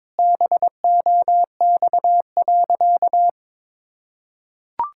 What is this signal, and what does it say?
Morse code, heard as a keyed tone, about 22 wpm, 700 Hz. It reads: BOX.
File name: practice / 22wpm / Sentences with 3-Word Phrases - 22wpm